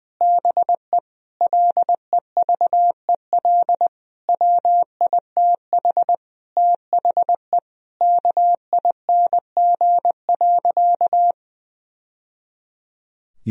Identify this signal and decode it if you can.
Morse code, heard as a keyed tone, about 20 wpm, 700 Hz. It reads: BE LEVEL WITH THE KING.
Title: Sentences from Top 1000 Words - Review Part 4 - 20wpm